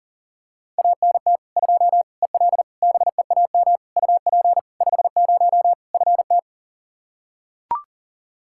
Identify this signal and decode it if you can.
Morse code, heard as a keyed tone, about 40 wpm, 700 Hz. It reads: ANT 2 EL BEAM UP 50 FT
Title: Practice QSO - Exchange 2 - Info - 40wpm